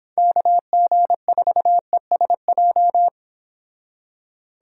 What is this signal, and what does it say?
Morse code, heard as a keyed tone, about 26 wpm, 700 Hz. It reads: KG4ESJ